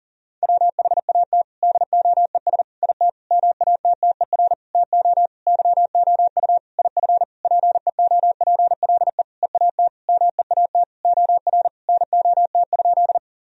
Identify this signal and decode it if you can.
Morse code, heard as a keyed tone, about 37 wpm, 700 Hz. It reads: WHAT DOES IT MATTER TO YOU IF PEOPLE EAT MEAT OR NOT?
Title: Sentences from Top 1000 Words - Review Part 3 - 40wpm